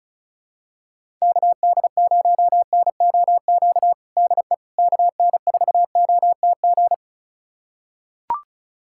Morse code, heard as a keyed tone, about 34 wpm, 700 Hz. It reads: KD0NOQ DE KN4OTG
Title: USA Callsign DE Callsign - 6 Characters - 35wpm